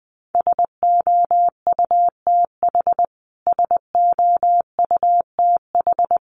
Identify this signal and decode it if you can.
Morse code, heard as a keyed tone, about 20 wpm, 700 Hz. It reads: SOUTH SOUTH
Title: Top 500 Words Repeated 20wpm